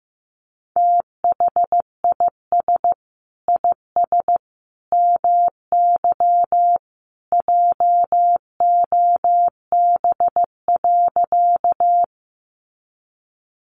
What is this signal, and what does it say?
Morse code, heard as a keyed tone, about 15 wpm, 700 Hz. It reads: THIS IS MY JOB.
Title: Sentences from Top 600 Words 15wpm